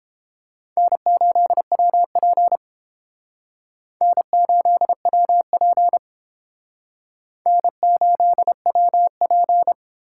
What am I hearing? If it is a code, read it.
Morse code, heard as a keyed tone, about 30 wpm, 700 Hz. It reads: N8WP N8WP N8WP